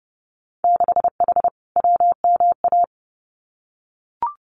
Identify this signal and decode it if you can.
Morse code, heard as a keyed tone, about 30 wpm, 700 Hz. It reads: 6H WMA
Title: Winter Field Day Exchanges Encore 30wpm